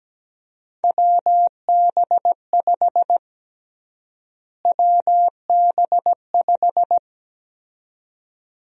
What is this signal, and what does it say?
Morse code, heard as a keyed tone, about 17 wpm, 700 Hz. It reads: WB5 WB5
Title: USA Call Sign Prefixes - Repeated 2x - 17wpm